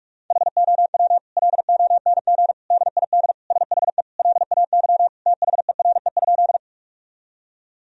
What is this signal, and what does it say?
Morse code, heard as a keyed tone, about 42 wpm, 700 Hz. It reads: HOW LONG DID SHE LAY THERE?
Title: Sentences from Top 1000 Words - Review Part 2 - 45wpm